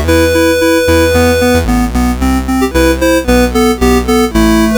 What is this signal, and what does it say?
There is no code.